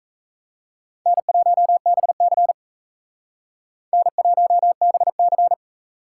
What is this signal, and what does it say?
Morse code, heard as a keyed tone, about 36 wpm, 700 Hz. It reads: N1BC N1BC